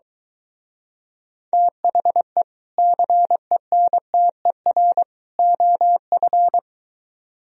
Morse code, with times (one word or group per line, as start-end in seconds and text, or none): 1.5-2.4 THE
2.8-5.0 CENTER
5.4-6.6 OF